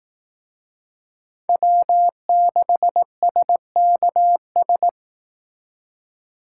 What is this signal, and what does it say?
Morse code, heard as a keyed tone, about 19 wpm, 700 Hz. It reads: W6SKS
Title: Speed Racing - USA 1x3 Call Signs 15wpm